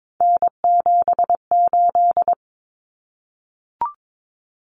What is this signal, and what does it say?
Morse code, heard as a keyed tone, about 22 wpm, 700 Hz. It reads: N78
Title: Common Cut Numbers - 3-Digit 22wpm